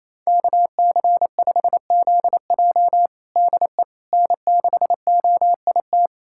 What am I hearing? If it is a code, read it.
Morse code, heard as a keyed tone, about 28 wpm, 700 Hz. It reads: KC5ZJ DE N6OIT
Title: USA Callsign DE Callsign - 5 Characters - 28wpm